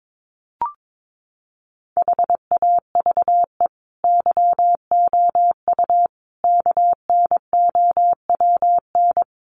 Morse code, heard as a keyed tone, about 22 wpm, 700 Hz. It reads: HAVE YOU KNOWN
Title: Sentences from Top 1300 Words 22wpm